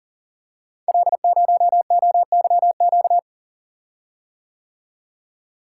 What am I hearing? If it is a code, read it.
Morse code, heard as a keyed tone, about 40 wpm, 700 Hz. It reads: L0OYQ
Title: Code Groups - 5 Letter:Numbers - 40wpm